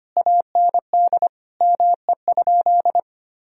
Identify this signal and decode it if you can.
Morse code, heard as a keyed tone, about 25 wpm, 700 Hz. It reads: AND ME?
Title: Sentences from Top 1000 Words - Review Part 2 - 25wpm